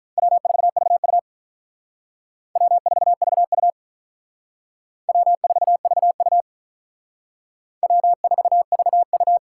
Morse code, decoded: W4VU W4VU W4VU W4VU

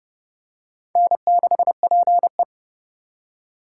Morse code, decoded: N6PE

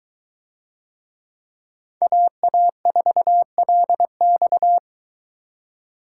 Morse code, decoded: AA4LX

